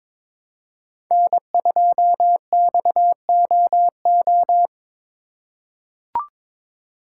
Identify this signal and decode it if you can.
Morse code, heard as a keyed tone, about 22 wpm, 700 Hz. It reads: N2XOO
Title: USA Call Signs - 5 Characters - 3x Character Spacing - 22wpm